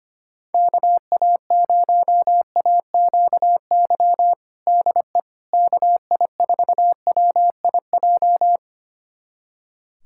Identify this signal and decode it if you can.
Morse code, heard as a keyed tone, about 25 wpm, 700 Hz. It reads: KA0AQY DE KI4WIJ
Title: USA Callsign DE Callsign - 6 Characters - 25wpm